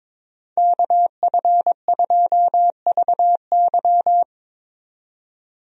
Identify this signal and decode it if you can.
Morse code, heard as a keyed tone, about 22 wpm, 700 Hz. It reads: KF2VY
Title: USA Call Signs - 5 Characters - 22wpm